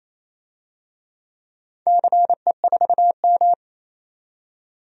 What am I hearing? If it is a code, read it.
Morse code, heard as a keyed tone, about 27 wpm, 700 Hz. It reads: CE4M
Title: DXCC-100 Call Signs - 4 Characters - 28wpm